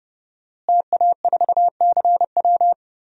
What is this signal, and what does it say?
Morse code, heard as a keyed tone, about 30 wpm, 700 Hz. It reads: TA4CW
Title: DXCC-100 Call Signs - 5 Characters - Repeated 2x - 30wpm